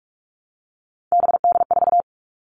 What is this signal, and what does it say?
Morse code, heard as a keyed tone, about 44 wpm, 700 Hz. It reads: BD4
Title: Callsign Prefixes for Top 100 DXCC - 45wpm